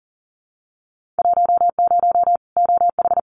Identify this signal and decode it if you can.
Morse code, heard as a keyed tone, about 40 wpm, 700 Hz. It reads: 10 OH
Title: Winter Field Day Exchanges 40wpm